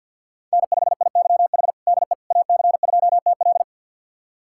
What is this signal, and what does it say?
Morse code, heard as a keyed tone, about 49 wpm, 700 Hz. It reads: N5IYH DE AC2TL
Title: USA Callsign DE Callsign - 5 Characters - 50wpm